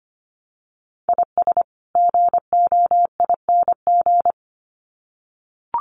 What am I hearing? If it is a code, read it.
Morse code, heard as a keyed tone, about 25 wpm, 700 Hz. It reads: IS GOING
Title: Sets of 2 Words 25wpm